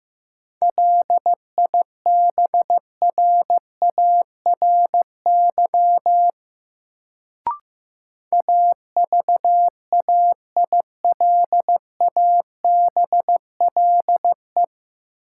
Morse code, heard as a keyed tone, about 15 wpm, 700 Hz. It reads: LIBRARY AVAILABLE
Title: Sentences from Top 600 Words - New Word Drill - 15wpm